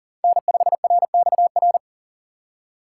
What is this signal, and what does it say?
Morse code, heard as a keyed tone, about 40 wpm, 700 Hz. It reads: N5RXR
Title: USA Call Signs - 5 Characters - 3x Character Spacing - 40wpm